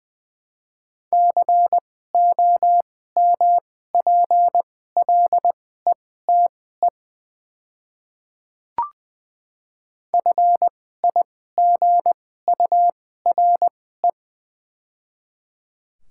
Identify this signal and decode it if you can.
Morse code, heard as a keyed tone, about 20 wpm, spaced as for 14 wpm, 700 Hz. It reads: COMPLETE FIGURE
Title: 2x Character Spacing - Top 500 Words - 20wpm